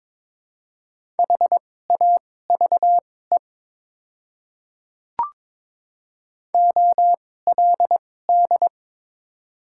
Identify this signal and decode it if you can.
Morse code, heard as a keyed tone, about 21 wpm, spaced as for 16 wpm, 700 Hz. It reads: HAVE OLD
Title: 2x Character Spacing - Top 300 Words - 22wpm